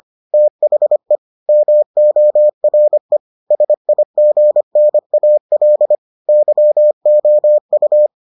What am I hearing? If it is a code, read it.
Morse code, heard as a keyed tone, about 25 wpm, 600 Hz. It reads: THE MORE SIGNAL YOU